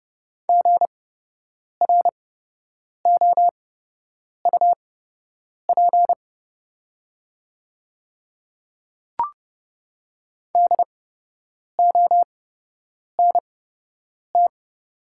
Morse code, heard as a keyed tone, about 30 wpm, spaced as for 8 wpm, 700 Hz. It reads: GROUP DONT
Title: 8x Character Spacing - Top 300 Words - 30wpm